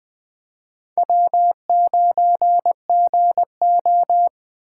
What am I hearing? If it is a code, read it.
Morse code, heard as a keyed tone, about 20 wpm, 700 Hz. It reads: W9GO
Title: Speed Racing - USA 1x2 Call Signs 17wpm